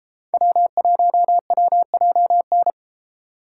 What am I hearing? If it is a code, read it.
Morse code, heard as a keyed tone, about 33 wpm, 700 Hz. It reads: W1WJN